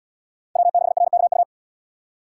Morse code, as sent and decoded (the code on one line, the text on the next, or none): .-- -.... .-. -.- ..-
W6RKU